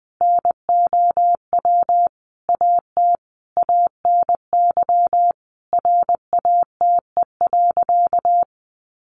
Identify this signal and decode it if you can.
Morse code, heard as a keyed tone, about 20 wpm, 700 Hz. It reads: NOW AT ANY RATE.
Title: Sentences from Top 900 Words 20wpm